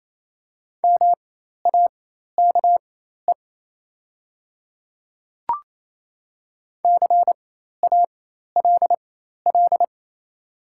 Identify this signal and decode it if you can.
Morse code, heard as a keyed tone, about 27 wpm, spaced as for 13 wpm, 690 Hz. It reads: MAKE CALL
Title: 4x Character Spacing - Top 100 Words - 28wpm